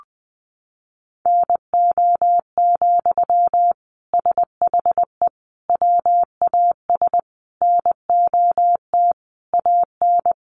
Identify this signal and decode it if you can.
Morse code, heard as a keyed tone, about 19 wpm, 700 Hz. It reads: NO, SHE WAS NOT AN
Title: Sentences from Top 800 Words 20wpm